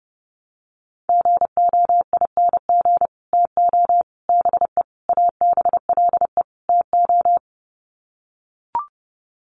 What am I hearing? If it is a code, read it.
Morse code, heard as a keyed tone, about 30 wpm, 700 Hz. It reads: GOING TO BE ABLE TO
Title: Sets of 5 Words 30wpm